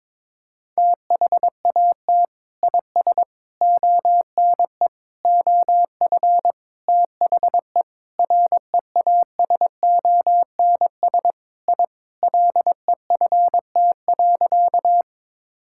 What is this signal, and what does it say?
Morse code, heard as a keyed tone, about 21 wpm, 700 Hz. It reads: THAT IS ONE OF THE REASONS I LEFT.
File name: practice / 22wpm / Sentences with 4-Word Phrases - 22wpm